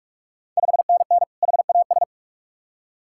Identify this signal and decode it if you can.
Morse code, heard as a keyed tone, about 44 wpm, 690 Hz. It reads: 5NN HAS
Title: Ontario QSO Party Exchange 45wpm